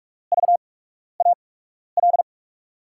Morse code, heard as a keyed tone, about 46 wpm, spaced as for 12 wpm, 700 Hz. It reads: VAL
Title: USA Call Signs - 6 Characters - 8x Character Spacing - 45wpm